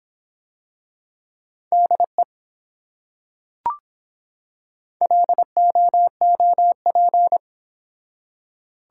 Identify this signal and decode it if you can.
Morse code, heard as a keyed tone, about 24 wpm, 700 Hz. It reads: DE LOOP